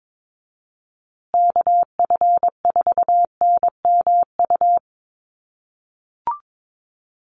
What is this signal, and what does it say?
Morse code, heard as a keyed tone, about 22 wpm, 700 Hz. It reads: KF4NMU
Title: USA Call Signs Repeated 3x 22wpm